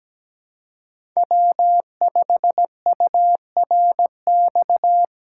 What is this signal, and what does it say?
Morse code, heard as a keyed tone, about 17 wpm, 700 Hz. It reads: W5URX
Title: USA Call Signs - 2x Character Spacing - 17wpm